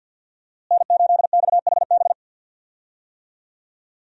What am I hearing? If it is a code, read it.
Morse code, heard as a keyed tone, about 53 wpm, 700 Hz. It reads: N8XHB
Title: USA Call Signs - 5 Characters - Repeated 2x - 50wpm